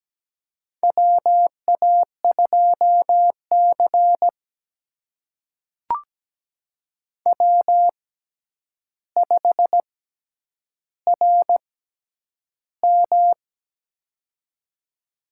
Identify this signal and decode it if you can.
Morse code, heard as a keyed tone, about 17 wpm, spaced as for 9 wpm, 700 Hz. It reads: WA2C W5RM
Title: USA Call Signs - 4 Characters - 6x Character Spacing - 17wpm